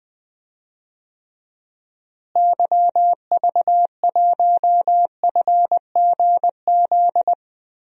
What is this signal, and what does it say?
Morse code, heard as a keyed tone, about 21 wpm, 700 Hz. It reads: YV1FGZ
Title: DXCC-100 Call Signs - 6 Characters - Repeated 3x - 20wpm